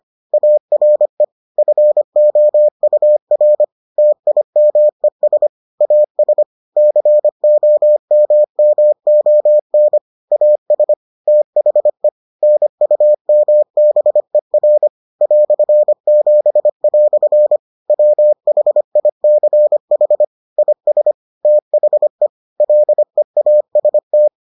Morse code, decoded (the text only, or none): ARE FOUR TIMES AS COMMON AS THE NUMBER "7" WHICH IS THE LEAST